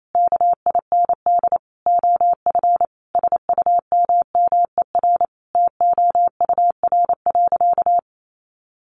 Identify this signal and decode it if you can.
Morse code, heard as a keyed tone, about 28 wpm, 700 Hz. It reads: KIND OF SUMMER TOUR.
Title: Sentences from Top 1400 Words 28wpm